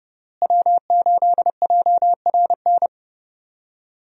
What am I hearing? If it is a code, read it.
Morse code, heard as a keyed tone, about 30 wpm, 700 Hz. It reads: W8JRN
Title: USA Call Signs - 5 Characters - Repeated 3x - 30wpm